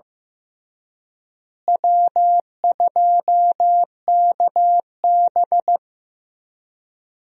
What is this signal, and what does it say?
Morse code, heard as a keyed tone, about 15 wpm, 700 Hz. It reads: W2KB
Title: USA Call Signs - 4 Characters - Repeated 2x - 15wpm